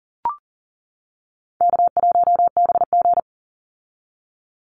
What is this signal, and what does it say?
Morse code, heard as a keyed tone, about 40 wpm, 700 Hz. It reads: K1BG
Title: Easy Call Signs - 40wpm